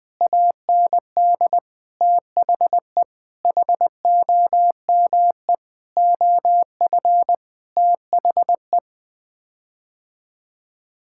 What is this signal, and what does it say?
Morse code, heard as a keyed tone, about 19 wpm, 700 Hz. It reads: AND THE HOME OF THE
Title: Sets of 9 Words - 20wpm